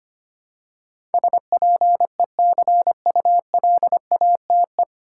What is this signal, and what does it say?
Morse code, heard as a keyed tone, about 25 wpm, 700 Hz. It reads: SPECULATE
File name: practice / 25wpm / 9 Letter Words - First 3 Letters Given - 25wpm